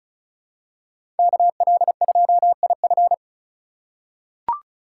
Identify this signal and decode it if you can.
Morse code, heard as a keyed tone, about 36 wpm, 690 Hz. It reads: KL2IF
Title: USA Call Signs - 5 Characters - 3x Character Spacing - 35wpm